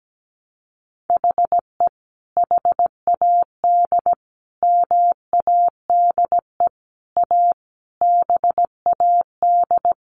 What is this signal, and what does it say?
Morse code, heard as a keyed tone, about 17 wpm, 700 Hz. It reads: HE HAD MADE A BAD
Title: Sentences from Top 1500 Words 17wpm